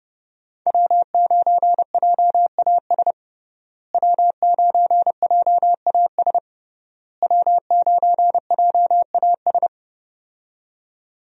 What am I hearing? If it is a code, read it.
Morse code, heard as a keyed tone, about 29 wpm, 700 Hz. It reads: W9JAS W9JAS W9JAS
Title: USA Call Signs - 5 Characters - Repeated 3x - 30wpm